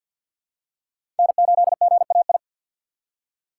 Morse code, heard as a keyed tone, about 51 wpm, 700 Hz. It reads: N8GAI
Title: USA Call Signs - 5 Characters - 2x Character Spacing - 50wpm